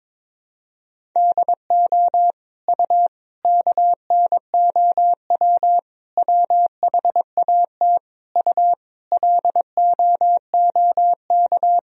Morse code, heard as a keyed tone, about 22 wpm, 700 Hz. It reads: DO U KNOW WHAT U LOOK
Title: Sentences from Top 100 Words and Abbreviations 22wpm